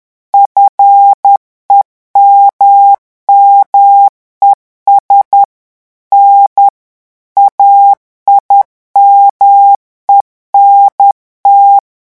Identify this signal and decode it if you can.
Morse code, heard as a keyed tone, about 10 wpm, 800 Hz. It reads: FEMMES N AIMENT
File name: morse mp3 / revision9 citations humoristiques vitesse 10 mots minute